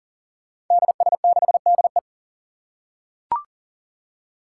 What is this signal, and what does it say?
Morse code, heard as a keyed tone, about 40 wpm, 700 Hz. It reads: DS6DE
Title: Five Challenging Letter-Number Combinations 40wpm